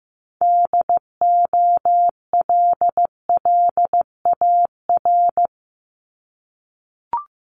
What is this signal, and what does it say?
Morse code, heard as a keyed tone, about 15 wpm, 700 Hz. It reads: DOLLAR
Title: Multi-letter Phonemes 15wpm